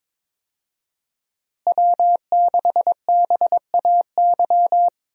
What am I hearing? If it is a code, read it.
Morse code, heard as a keyed tone, about 22 wpm, 700 Hz. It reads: W6BAY
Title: USA Call Signs - 5 Characters - Repeated 2x - 22wpm